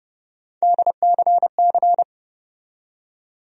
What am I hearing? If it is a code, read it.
Morse code, heard as a keyed tone, about 30 wpm, 700 Hz. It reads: DCC
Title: USA Call Sign Postfixes - 3 Letters 30wpm